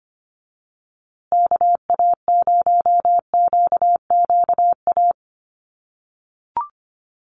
Morse code, heard as a keyed tone, about 25 wpm, 700 Hz. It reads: KA0QQA
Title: USA Call Signs Repeated 3x 25wpm